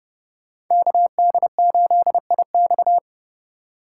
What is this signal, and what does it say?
Morse code, heard as a keyed tone, about 30 wpm, 700 Hz. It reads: KD8IX
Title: USA Call Signs - 4x Character Spacing - 30wpm